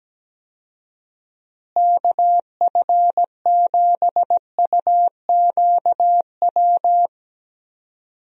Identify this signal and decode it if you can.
Morse code, heard as a keyed tone, about 17 wpm, 700 Hz. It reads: KF7UQW
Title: USA Call Signs - 6 Characters - Repeated 2x - 17wpm